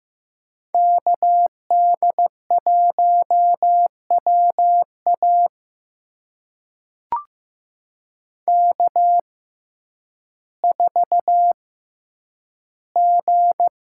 Morse code, heard as a keyed tone, about 15 wpm, spaced as for 9 wpm, 700 Hz. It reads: KD1WA K4G
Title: USA Call Signs - 5 Characters - 6x Character Spacing - 15wpm